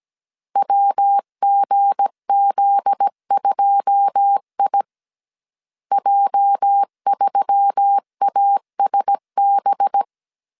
Morse code, heard as a keyed tone, about 17 wpm, 800 Hz. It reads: WGZ2I J3ASB
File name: CW / DET / 1 9 1500C S C 16wpm 2909